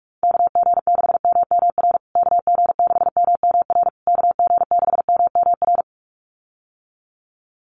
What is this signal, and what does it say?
Morse code, heard as a keyed tone, about 44 wpm, 700 Hz. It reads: KG6MMR KG6MMR KG6MMR
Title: USA Call Signs Repeated 3x 45wpm